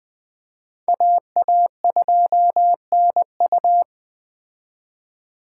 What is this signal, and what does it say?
Morse code, heard as a keyed tone, about 20 wpm, 700 Hz. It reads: AA2NU
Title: Speed Racing - USA 2x2 Call Signs 17wpm